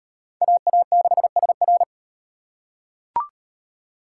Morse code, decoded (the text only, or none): AA6SR